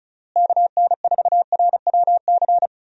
Morse code, heard as a keyed tone, about 36 wpm, 700 Hz. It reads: KN4RWC
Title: USA Call Signs - 6 Characters - Repeated 2x - 35wpm